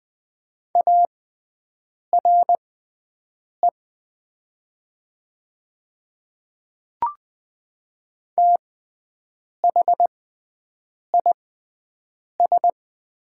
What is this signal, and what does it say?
Morse code, heard as a keyed tone, about 20 wpm, spaced as for 7 wpm, 690 Hz. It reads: ARE THIS
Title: 6x Character Spacing - Top 100 Words - 20wpm